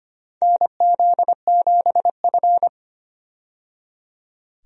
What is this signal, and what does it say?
Morse code, heard as a keyed tone, about 25 wpm, 700 Hz. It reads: NZ7F
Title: USA Call Signs - 4 Characters - 25wpm